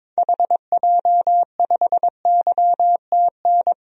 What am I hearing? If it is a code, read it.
Morse code, heard as a keyed tone, about 22 wpm, 700 Hz. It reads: HJ5YTN
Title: DXCC-100 Call Signs - 6 Characters - 22wpm